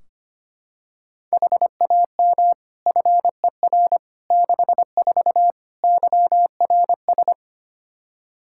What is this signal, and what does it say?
Morse code, heard as a keyed tone, about 25 wpm, 700 Hz. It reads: HAM FER 64 YRS
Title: K1USN SST QSO Practice 25wpm